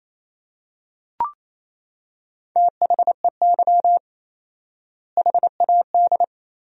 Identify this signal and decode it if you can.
Morse code, heard as a keyed tone, about 28 wpm, 700 Hz. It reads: THEY HAD
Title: 4x Word-Spacing - Sentences from Top 400 Words - 28wpm